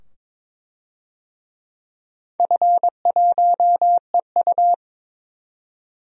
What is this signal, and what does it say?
Morse code, heard as a keyed tone, about 22 wpm, 700 Hz. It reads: F1EU
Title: DXCC-100 Call Signs - 4 Characters - 22wpm